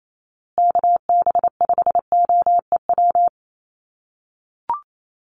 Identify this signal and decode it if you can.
Morse code, heard as a keyed tone, about 28 wpm, 700 Hz. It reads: KB5OEW
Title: USA Call Signs Repeated 3x 28wpm